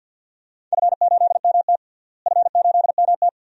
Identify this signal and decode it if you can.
Morse code, heard as a keyed tone, about 50 wpm, 700 Hz. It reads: F8MT F8MT
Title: DXCC-100 Call Signs - 4 Characters - Repeated 2x - 50wpm